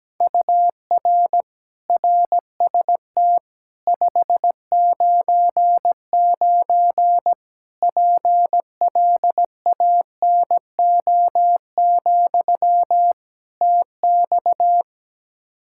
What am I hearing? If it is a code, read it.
Morse code, heard as a keyed tone, about 17 wpm, 700 Hz. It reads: UR RST 599 PLANO, TX
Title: K1USN SST QSO Practice 17wpm